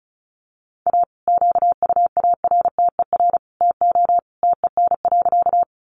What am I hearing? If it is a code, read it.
Morse code, heard as a keyed tone, about 34 wpm, 700 Hz. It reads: A QUARTER TO TEN.
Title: Sentences from Top 1300 Words 35wpm